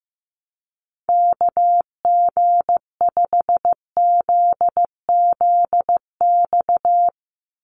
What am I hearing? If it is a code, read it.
Morse code, heard as a keyed tone, about 15 wpm, 700 Hz. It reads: KG5ZZX
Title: USA Call Signs 15wpm